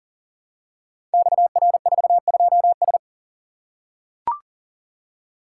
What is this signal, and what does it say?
Morse code, heard as a keyed tone, about 38 wpm, 700 Hz. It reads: XR42S
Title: Code Groups - 5 Letter:Numbers - 40wpm